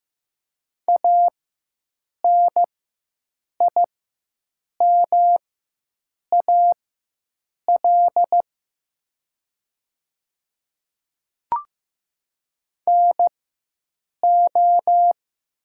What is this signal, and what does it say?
Morse code, heard as a keyed tone, about 15 wpm, spaced as for 7 wpm, 700 Hz. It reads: ANIMAL NO
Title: 4x Character Spacing - Top 200 Words - 15wpm